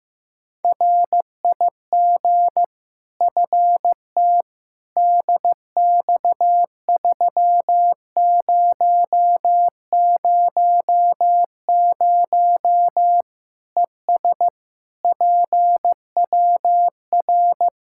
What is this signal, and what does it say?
Morse code, heard as a keyed tone, about 15 wpm, 700 Hz. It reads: RIG FT DX3000 ES PWR
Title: K1USN SST QSO Practice 15wpm